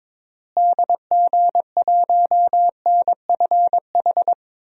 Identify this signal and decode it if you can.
Morse code, heard as a keyed tone, about 22 wpm, 700 Hz. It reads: DG1NFH
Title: DXCC-100 Call Signs - 6 Characters - Repeated 3x - 22wpm